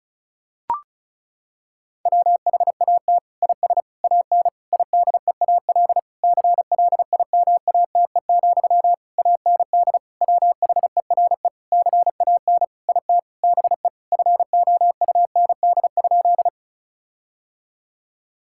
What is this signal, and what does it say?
Morse code, heard as a keyed tone, about 34 wpm, 700 Hz. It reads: WHAT IS AN IDEAL CLIMATE, AND WHERE CAN IT BE FOUND?
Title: Sentences from Top 1600 Words 35wpm